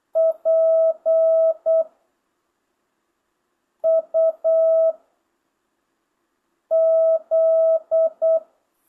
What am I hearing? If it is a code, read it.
Morse code, heard as a keyed tone, about 8 wpm, 650 Hz. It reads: P U Z
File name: jh8cbh / 2025cw